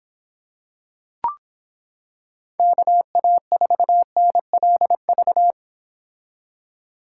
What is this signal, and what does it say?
Morse code, heard as a keyed tone, about 26 wpm, 700 Hz. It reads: KA4NLV